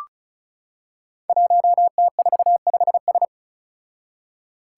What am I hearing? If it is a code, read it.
Morse code, heard as a keyed tone, about 34 wpm, 700 Hz. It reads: 1T45S
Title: Five Challenging Letter-Number Combinations 35wpm